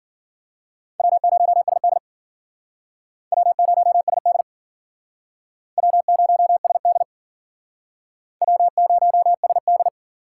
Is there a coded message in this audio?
Yes: Morse code from 1.0 to 9.9 s, heard as a keyed tone, about 50 wpm, 700 Hz. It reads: W0SD W0SD W0SD W0SD